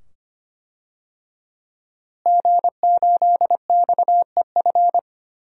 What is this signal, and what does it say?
Morse code, heard as a keyed tone, about 25 wpm, 700 Hz. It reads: G8XEF
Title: DXCC-100 Call Signs - 5 Characters - Repeated 2x - 25wpm